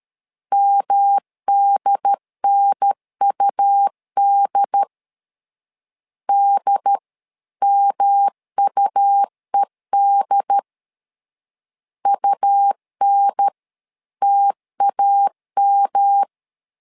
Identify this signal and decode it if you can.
Morse code, heard as a keyed tone, about 13 wpm, 800 Hz. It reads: MDNUD DMUED UNTAM
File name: LEZIONI / 3 MWRO / 3 M 12wpm 1541